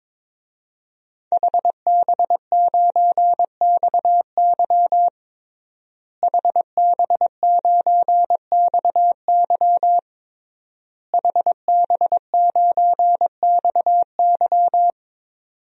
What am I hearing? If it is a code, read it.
Morse code, heard as a keyed tone, about 22 wpm, 700 Hz. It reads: HB9XY HB9XY HB9XY
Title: DXCC-100 Call Signs - 5 Characters - Repeated 3x - 22wpm